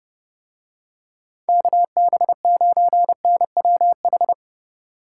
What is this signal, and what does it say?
Morse code, heard as a keyed tone, about 29 wpm, 700 Hz. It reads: KB9NWH